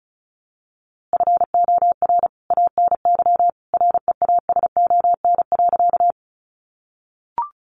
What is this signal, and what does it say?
Morse code, heard as a keyed tone, about 34 wpm, 700 Hz. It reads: FOR ANY REASON.
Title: A Sound Of Thunder 35wpm